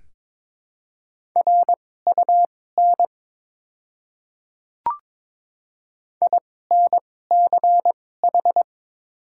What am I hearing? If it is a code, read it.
Morse code, heard as a keyed tone, about 21 wpm, spaced as for 16 wpm, 700 Hz. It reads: RUN INCH
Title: 2x Character Spacing - Top 500 Words - 22wpm